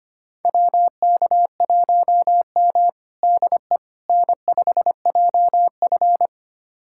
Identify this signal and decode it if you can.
Morse code, heard as a keyed tone, about 25 wpm, 700 Hz. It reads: WK1M DE N5JF
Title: USA Callsign DE Callsign - 4 Characters - 25wpm